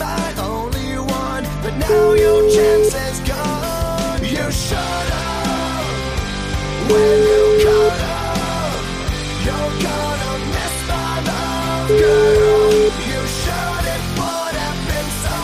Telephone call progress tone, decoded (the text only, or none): ringback tone